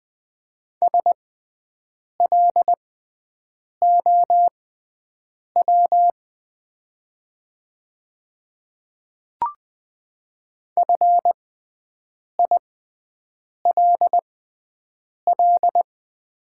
Morse code, heard as a keyed tone, about 20 wpm, spaced as for 7 wpm, 700 Hz. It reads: SLOW FILL
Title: 6x Character Spacing - Top 500 Words - 20wpm